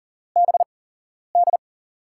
Morse code, heard as a keyed tone, about 40 wpm, spaced as for 11 wpm, 700 Hz. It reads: BD
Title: USA Call Signs - 8x Character Spacing - 40wpm